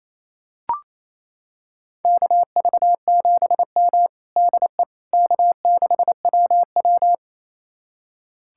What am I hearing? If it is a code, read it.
Morse code, heard as a keyed tone, about 28 wpm, 700 Hz. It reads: KV7M DE K6WW
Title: USA Callsign DE Callsign - 4 Characters - 28wpm